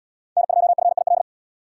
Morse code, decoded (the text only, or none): N2FIL